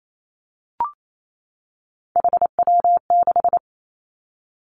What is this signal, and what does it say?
Morse code, heard as a keyed tone, about 28 wpm, 690 Hz. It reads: HW6
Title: Three Challenging Letter-Number Combinations 28wpm